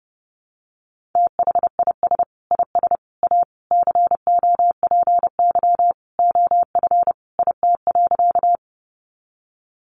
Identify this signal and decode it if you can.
Morse code, heard as a keyed tone, about 30 wpm, 700 Hz. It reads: THIS IS A COPY OF IT.
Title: Sentences from Top 1400 Words 30wpm